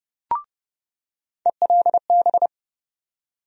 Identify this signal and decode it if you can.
Morse code, heard as a keyed tone, about 30 wpm, 700 Hz. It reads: ELB
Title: USA Call Sign Postfixes - 3 Letters 30wpm